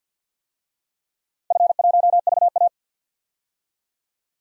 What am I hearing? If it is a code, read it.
Morse code, heard as a keyed tone, about 51 wpm, 700 Hz. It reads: F1VA
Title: DXCC-100 Call Signs - 4 Characters - 50wpm